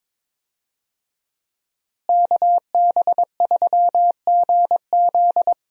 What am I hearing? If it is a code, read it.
Morse code, heard as a keyed tone, about 22 wpm, 700 Hz. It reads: KB3GZ
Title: USA Call Signs - 5 Characters - Repeated 2x - 22wpm